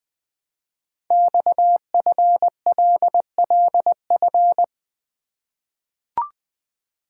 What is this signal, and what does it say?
Morse code, heard as a keyed tone, about 20 wpm, 700 Hz. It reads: XFLLF
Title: Code Groups - 5 Letters - 20wpm